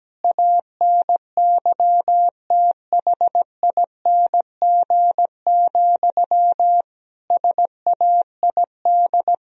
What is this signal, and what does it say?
Morse code, heard as a keyed tone, about 17 wpm, 700 Hz. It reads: ANYTHING, SAID
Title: Sentences from Top 1000 Words - Review Part 4 - 17wpm